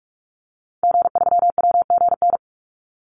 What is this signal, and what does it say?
Morse code, heard as a keyed tone, about 44 wpm, 700 Hz. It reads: G3WGN
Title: Call Signs - 45wpm